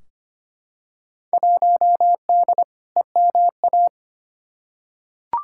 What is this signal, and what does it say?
Morse code, heard as a keyed tone, about 25 wpm, 700 Hz. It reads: 1D EMA
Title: Field Day Exchanges - 25wpm